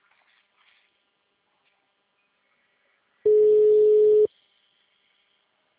Ringback tone (Europe, 425 Hz).